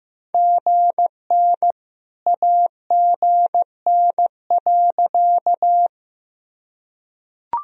Morse code, heard as a keyed tone, about 15 wpm, 700 Hz. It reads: GN AGN.
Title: Sentences from Top 300 Words and Abbreviations 15wpm